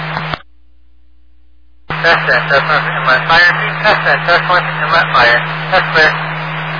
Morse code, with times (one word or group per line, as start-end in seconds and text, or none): none